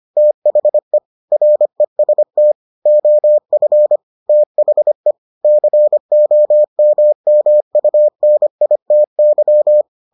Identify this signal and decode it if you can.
Morse code, heard as a keyed tone, about 25 wpm, 600 Hz. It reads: THE REST OF THE COMMUNITY